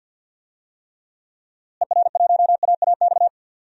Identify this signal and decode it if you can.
Morse code, heard as a keyed tone, about 50 wpm, 700 Hz. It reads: ER1AAX
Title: DXCC-100 Call Signs - 6 Characters - Repeated 2x - 50wpm